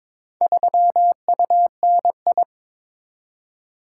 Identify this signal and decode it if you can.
Morse code, heard as a keyed tone, about 22 wpm, 700 Hz. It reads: 3UNI
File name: practice / 22wpm / Four Letter-Number Combinations - 22wpm